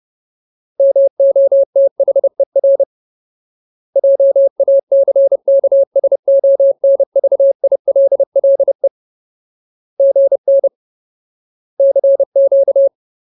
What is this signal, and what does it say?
Morse code, heard as a keyed tone, about 30 wpm, 550 Hz. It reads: MOTHER JACKSONVILLE GN CQ
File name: Ultimate Word List / 30WPM Ultimate